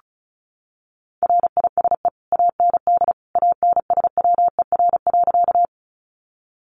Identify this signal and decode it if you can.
Morse code, heard as a keyed tone, about 34 wpm, 700 Hz. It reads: RISE AND ANSWER.
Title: Sentences from Top 900 Words 35wpm